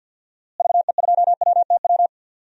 Morse code, heard as a keyed tone, about 51 wpm, 700 Hz. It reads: VE2WTW
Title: DXCC-100 Call Signs - 6 Characters - Repeated 2x - 50wpm